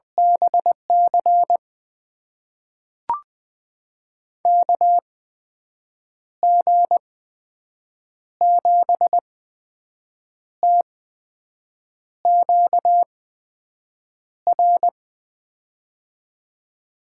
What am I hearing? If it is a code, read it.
Morse code, heard as a keyed tone, about 20 wpm, spaced as for 6 wpm, 700 Hz. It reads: BC KG7TQR
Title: USA Call Signs - 6 Characters - 8x Character Spacing - 20wpm